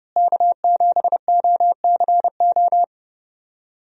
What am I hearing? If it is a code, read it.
Morse code, heard as a keyed tone, about 30 wpm, 700 Hz. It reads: K7OCO
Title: USA Call Signs - 5 Characters - Repeated 3x - 30wpm